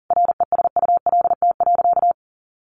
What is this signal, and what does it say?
Morse code, heard as a keyed tone, about 40 wpm, 700 Hz. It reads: RESULT.
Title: Sentences from Top 700 Words 40wpm